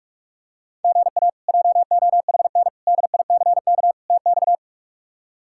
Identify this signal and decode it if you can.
Morse code, heard as a keyed tone, about 46 wpm, 700 Hz. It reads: GA JOHN DICK TX
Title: K1USN SST QSO Practice - 45wpm